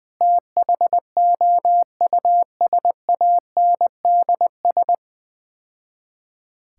Morse code, heard as a keyed tone, about 20 wpm, 700 Hz. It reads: THOUSANDS
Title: Build A Phrase - 4-Word Phrases - 20wpm